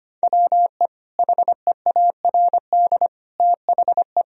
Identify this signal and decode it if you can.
Morse code, heard as a keyed tone, about 25 wpm, 700 Hz. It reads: WE HEARD THE
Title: Sentences from Top 1000 Words - Review Part 1 - 25wpm